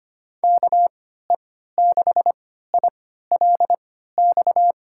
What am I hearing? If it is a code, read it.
Morse code, heard as a keyed tone, about 25 wpm, spaced as for 14 wpm, 700 Hz. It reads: KE6ILX
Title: USA Call Signs - 6 Characters - 3x Character Spacing - 25wpm